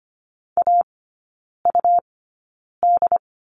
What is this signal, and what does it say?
Morse code, heard as a keyed tone, about 25 wpm, spaced as for 9 wpm, 700 Hz. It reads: AUD
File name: Lesson 17e - Letter u - Three Characters 25wpm